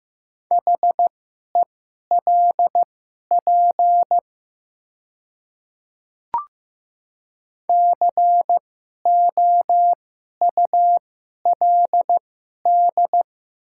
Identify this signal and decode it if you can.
Morse code, heard as a keyed tone, about 15 wpm, spaced as for 11 wpm, 700 Hz. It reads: HELP COULD
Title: 2x Character Spacing - Top 200 Words - 15wpm